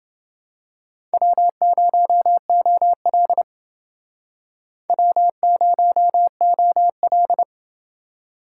Morse code, heard as a keyed tone, about 28 wpm, 700 Hz. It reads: W0OL W0OL